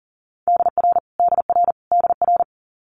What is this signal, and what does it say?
Morse code, heard as a keyed tone, about 37 wpm, 690 Hz. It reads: DR DR DR
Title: Multi-letter Phonemes Repeated 3x 40wpm